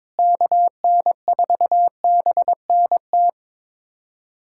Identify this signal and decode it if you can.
Morse code, heard as a keyed tone, about 22 wpm, 700 Hz. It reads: KN4BNT
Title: USA Callsign DE Callsign - 6 Characters - 22wpm